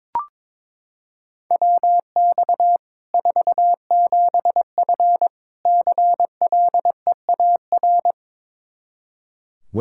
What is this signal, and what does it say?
Morse code, heard as a keyed tone, about 22 wpm, 700 Hz. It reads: WX 47F CLEAR
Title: Practice QSO - Exchange 2 - Info - 22wpm